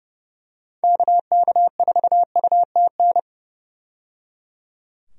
Morse code, heard as a keyed tone, about 30 wpm, 700 Hz. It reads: KK4UTN
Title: USA Call Signs - 6 Characters - 30wpm